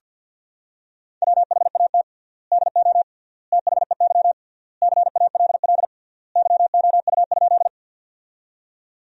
Morse code, heard as a keyed tone, about 46 wpm, 700 Hz. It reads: WHAT DO THEY CALL YOU?
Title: 3x Word-Spacing - Sentences from Top 100 Words - 50wpm